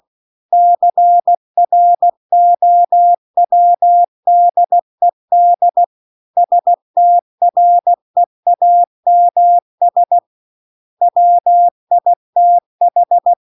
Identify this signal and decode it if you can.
Morse code, heard as a keyed tone, about 16 wpm, 700 Hz. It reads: CROWDED STREAMS WITH
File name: CW-The-Big-Twoheaded-River-2-16WPM